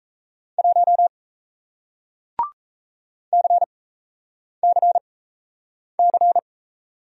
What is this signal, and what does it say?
Morse code, heard as a keyed tone, about 37 wpm, spaced as for 8 wpm, 700 Hz. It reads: 1 CCC